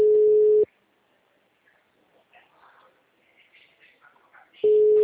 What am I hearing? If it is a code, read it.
Ringback tone (Europe, 425 Hz).
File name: out- -2012